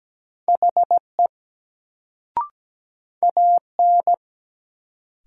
Morse code, heard as a keyed tone, about 17 wpm, 700 Hz. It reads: HE AN
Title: Sets of 2 Words - New Word Drill - 17wpm